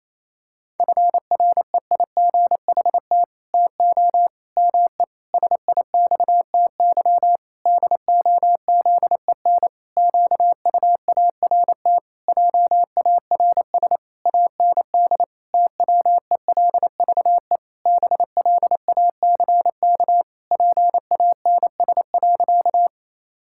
Morse code, heard as a keyed tone, about 28 wpm, 700 Hz. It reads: FREIGHT TO ME SIXTY DOZEN QUART JARS AND TWELVE BLACK PANS.
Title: Pangram Challenge v2 - 28wpm